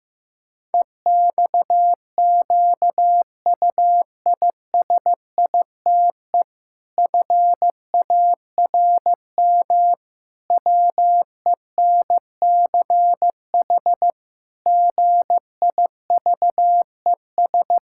Morse code, heard as a keyed tone, about 15 wpm, 700 Hz. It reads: EXQUISITE FARM WENCH GIVES
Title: Pangram Challenge v2 - 15wpm